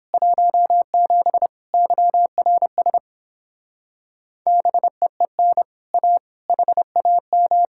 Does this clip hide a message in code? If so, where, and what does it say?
Yes: Morse code from 0.1 to 7.7 s, heard as a keyed tone, about 29 wpm, 700 Hz. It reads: 17 YRS BEEN A HAM